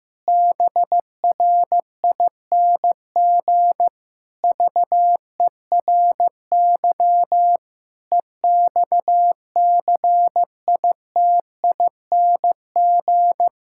Morse code, heard as a keyed tone, about 16 wpm, 700 Hz. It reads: BRING VERY EXCITING
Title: Pangram Challenge v2 - 15wpm